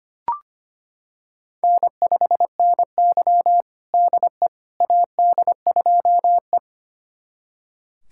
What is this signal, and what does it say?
Morse code, heard as a keyed tone, about 25 wpm, 700 Hz. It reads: N5NY DE AD2E
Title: USA Callsign DE Callsign - 4 Characters - 25wpm